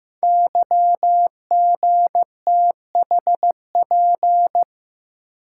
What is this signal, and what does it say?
Morse code, heard as a keyed tone, about 15 wpm, 700 Hz. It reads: YGTHP
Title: Code Groups - 5 Letters - 15wpm